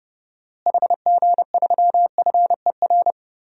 Morse code, heard as a keyed tone, about 31 wpm, 700 Hz. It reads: HG3FER
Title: DXCC-100 Call Signs - 6 Characters - Repeated 2x - 30wpm